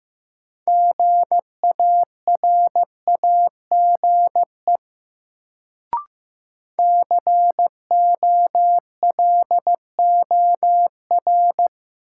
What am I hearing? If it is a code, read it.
Morse code, heard as a keyed tone, about 16 wpm, 700 Hz. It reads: GARAGE COLOR
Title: Sentences from Top 1600 Words - New Word Drill Alt - 15wpm